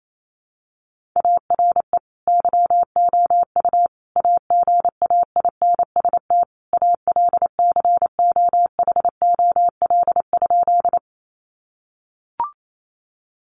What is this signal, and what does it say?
Morse code, heard as a keyed tone, about 28 wpm, 700 Hz. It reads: ARE YOU AGAINST ALCOHOL?
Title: Sentences from Top 1400 Words 28wpm